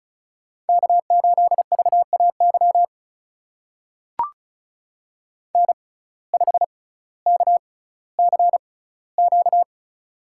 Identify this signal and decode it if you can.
Morse code, heard as a keyed tone, about 36 wpm, spaced as for 18 wpm, 700 Hz. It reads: K8VAY N5KCQ
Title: USA Call Signs - 5 Characters - 6x Character Spacing - 35wpm